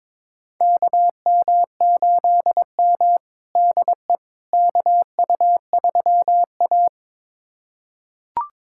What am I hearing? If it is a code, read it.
Morse code, heard as a keyed tone, about 22 wpm, 700 Hz. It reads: KM8M DE KU3A
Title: USA Callsign DE Callsign - 4 Characters - 22wpm